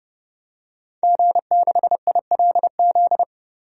Morse code, heard as a keyed tone, about 30 wpm, 700 Hz. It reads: G6ILZ
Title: DXCC-100 Call Signs - 5 Characters - 30wpm